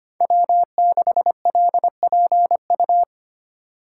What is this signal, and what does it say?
Morse code, heard as a keyed tone, about 25 wpm, 700 Hz. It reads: W6LPU
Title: USA Call Signs - 5 Characters - 2x Character Spacing - 25wpm